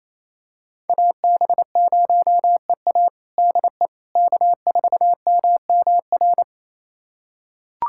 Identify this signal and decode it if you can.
Morse code, heard as a keyed tone, about 28 wpm, 700 Hz. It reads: AB0EA DE K4MMR
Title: USA Callsign DE Callsign - 5 Characters - 28wpm